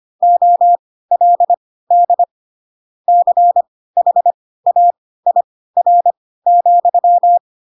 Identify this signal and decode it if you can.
Morse code, heard as a keyed tone, about 25 wpm, spaced as for 16 wpm, 700 Hz. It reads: OLD CHAIR,
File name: Pinocchio / Pinocchio ch3 10 25 ik1hge